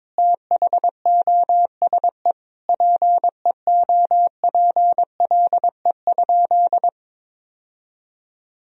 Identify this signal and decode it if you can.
Morse code, heard as a keyed tone, about 21 wpm, 700 Hz. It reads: THOSE PEOPLE?
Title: Sentences from Top 1000 Words - Review Part 1 - 22wpm